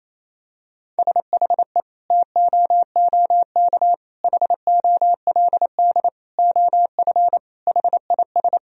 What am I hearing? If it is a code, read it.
Morse code, heard as a keyed tone, about 28 wpm, 700 Hz. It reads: SHE TOOK HOLD OF HIS
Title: Sentences from Top 1000 Words - Review Part 2 - 28wpm